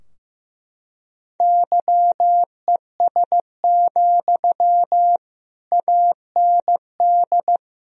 Morse code, heard as a keyed tone, about 15 wpm, 700 Hz. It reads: YES, AND
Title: Sentences from Top 1000 Words - Review Part 3 - 15wpm